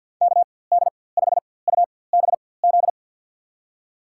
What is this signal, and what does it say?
Morse code, heard as a keyed tone, about 50 wpm, spaced as for 23 wpm, 700 Hz. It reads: KD5UBZ
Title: USA Call Signs - 6 Characters - 4x Character Spacing - 50wpm